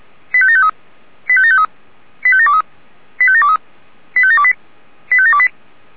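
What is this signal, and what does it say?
Five-tone selective call (ZVEI), heard as quick runs of five tones, tones of about 70 ms. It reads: 85753 85753 85723 85723 85728 85728
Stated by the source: Five-tone ZVEI selective call 85726, received on SDR.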